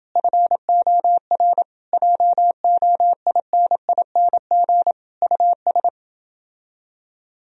Morse code, heard as a keyed tone, about 29 wpm, 700 Hz. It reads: FOR JOINING US